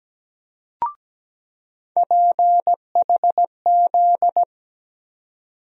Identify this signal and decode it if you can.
Morse code, heard as a keyed tone, about 17 wpm, 700 Hz. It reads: PHZ
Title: USA Call Sign Postfixes - 3 Letters 17wpm